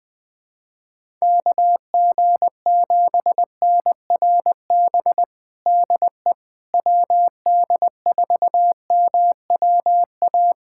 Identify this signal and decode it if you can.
Morse code, heard as a keyed tone, about 21 wpm, 700 Hz. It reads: KG7NRB DE WD4MWA
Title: USA Callsign DE Callsign 20wpm